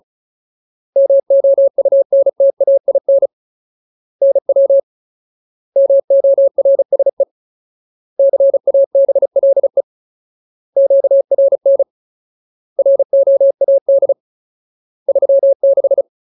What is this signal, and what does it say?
Morse code, heard as a keyed tone, about 34 wpm, 550 Hz. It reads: MOUNTAIN NW MORSE CABLE QRN ROAD 36
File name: Ultimate Word List / 35WPM Ultimate